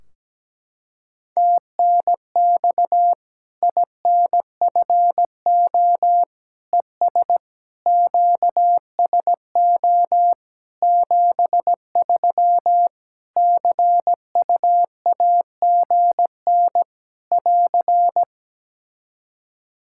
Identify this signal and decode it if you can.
Morse code, heard as a keyed tone, about 17 wpm, 700 Hz. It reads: TNX INFO ES QSO 73 CUAGN <AR>
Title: K1USN SST QSO Practice 17wpm